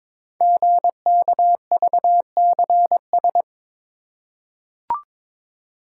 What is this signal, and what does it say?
Morse code, heard as a keyed tone, about 23 wpm, 700 Hz. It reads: GKVCS
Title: Code Groups - 5 Letters - 22wpm